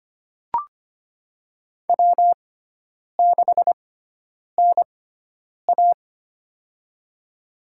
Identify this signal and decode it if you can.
Morse code, heard as a keyed tone, about 25 wpm, spaced as for 9 wpm, 700 Hz. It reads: W6NA
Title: USA Call Signs - 4 Characters - 6x Character Spacing - 25wpm